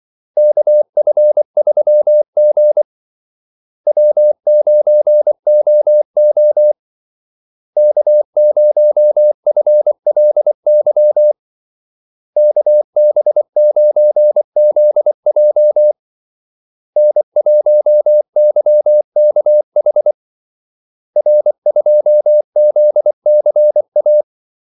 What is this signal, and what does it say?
Morse code, heard as a keyed tone, about 24 wpm, 600 Hz. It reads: KF3G W9OO K0FLY KB9ZJ N1YKH R2ZCA
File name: Callsigns-24wpm